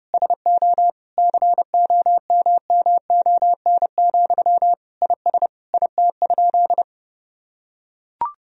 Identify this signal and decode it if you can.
Morse code, heard as a keyed tone, about 30 wpm, 700 Hz. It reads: SO COMMON, IS IT?
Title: Sentences from Top 1000 Words - Review Part 3 - 30wpm